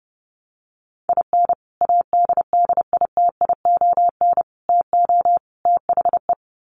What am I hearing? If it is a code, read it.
Morse code, heard as a keyed tone, about 30 wpm, 700 Hz. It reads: IN ADDITION TO THE
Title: Sets of 4 Words 30wpm